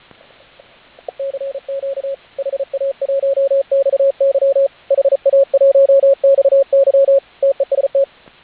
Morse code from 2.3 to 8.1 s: HA1XY HA1XY TEST